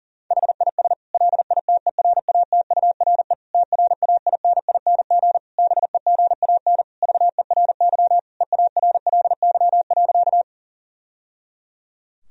Morse code, heard as a keyed tone, about 40 wpm, 700 Hz. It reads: HIS LITERATURE TRAINING BEGAN VERY EARLY.
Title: Sentences from Top 1000 Words - Review Part 4 - 40wpm